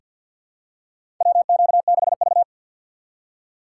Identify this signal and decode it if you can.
Morse code, heard as a keyed tone, about 50 wpm, 700 Hz. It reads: WQ6V
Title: USA Call Signs - 4 Characters - 3x Character Spacing - 50wpm